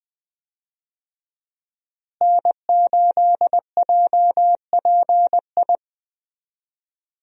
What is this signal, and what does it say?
Morse code, heard as a keyed tone, about 20 wpm, 700 Hz. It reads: N8JPI
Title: USA Call Signs - 5 Characters - Repeated 2x - 20wpm